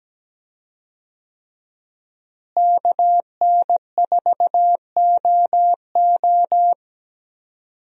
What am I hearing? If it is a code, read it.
Morse code, heard as a keyed tone, about 17 wpm, 700 Hz. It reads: KN4OO
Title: USA Call Signs - 5 Characters - Repeated 3x - 17wpm